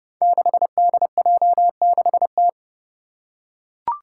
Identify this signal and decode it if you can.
Morse code, heard as a keyed tone, about 30 wpm, 700 Hz. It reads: 6DJ6T
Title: Five Challenging Letter-Number Combinations 30wpm